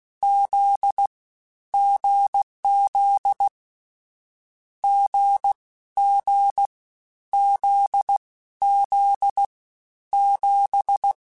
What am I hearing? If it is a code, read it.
Morse code, heard as a keyed tone, about 16 wpm, 800 Hz. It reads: Z GZ G G Z Z 7
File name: Short Runs / Short Runs-mp3s / awj1ndb6gz7 @15